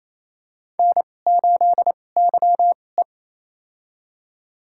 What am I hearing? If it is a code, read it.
Morse code, heard as a keyed tone, about 28 wpm, spaced as for 20 wpm, 700 Hz. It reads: N8YE
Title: USA Call Signs - 4 Characters - 2x Character Spacing - 28wpm